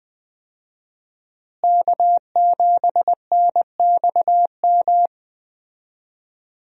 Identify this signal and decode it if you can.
Morse code, heard as a keyed tone, about 21 wpm, 700 Hz. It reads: K7NXM
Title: USA Call Signs - 5 Characters - Repeated 2x - 20wpm